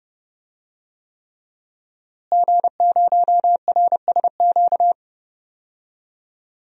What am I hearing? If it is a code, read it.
Morse code, heard as a keyed tone, about 30 wpm, 700 Hz. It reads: G0RSQ
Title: DXCC-100 Call Signs - 5 Characters - Repeated 3x - 30wpm